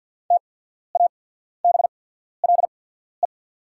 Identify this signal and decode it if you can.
Morse code, heard as a keyed tone, about 48 wpm, spaced as for 14 wpm, 690 Hz. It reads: TABLE
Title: 8x Character Spacing - Top 500 Words - 50wpm